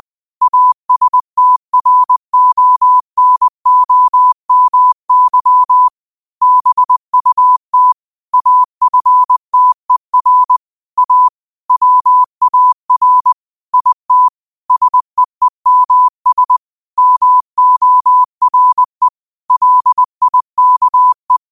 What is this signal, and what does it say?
Morse code, heard as a keyed tone, about 20 wpm, 1000 Hz. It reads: ASTRONOMY BUT AFTER A WAR IT SEEMS MORE LIKE